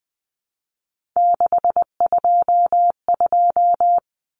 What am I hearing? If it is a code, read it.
Morse code, heard as a keyed tone, about 20 wpm, 700 Hz. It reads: 622
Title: Common Cut Numbers - 3-Digit 20wpm